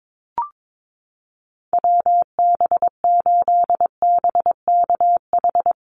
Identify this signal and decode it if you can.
Morse code, heard as a keyed tone, about 22 wpm, 700 Hz. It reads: WB8BKH
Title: USA Call Signs Repeated 3x 22wpm